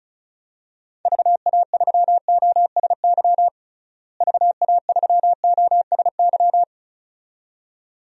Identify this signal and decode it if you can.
Morse code, heard as a keyed tone, about 36 wpm, 700 Hz. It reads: VA3OSY VA3OSY
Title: DXCC-100 Call Signs - 6 Characters - Repeated 2x - 35wpm